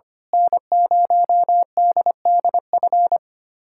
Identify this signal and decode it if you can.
Morse code, heard as a keyed tone, about 25 wpm, 700 Hz. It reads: N0DDF
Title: USA Callsign DE Callsign 25wpm